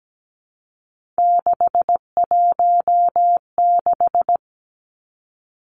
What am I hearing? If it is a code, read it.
Morse code, heard as a keyed tone, about 17 wpm, 700 Hz. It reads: 616
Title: Three Challenging Letter-Number Combinations 17wpm